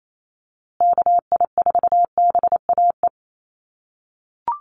Morse code, heard as a keyed tone, about 28 wpm, 700 Hz. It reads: KI4BAE
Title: USA Call Signs Repeated 2x 28wpm